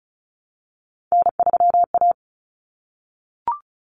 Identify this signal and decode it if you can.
Morse code, heard as a keyed tone, about 34 wpm, 700 Hz. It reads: N3A
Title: Common Cut Numbers - 3-Digit 35wpm